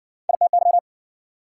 Morse code, heard as a keyed tone, about 53 wpm, 700 Hz. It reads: ITY